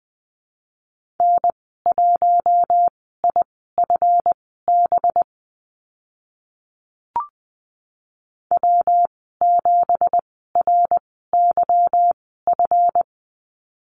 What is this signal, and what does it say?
Morse code, heard as a keyed tone, about 20 wpm, spaced as for 14 wpm, 700 Hz. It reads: N1IFB W7RYF
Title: 186 Lesson 30g - Letter b - Call Signs 20wpm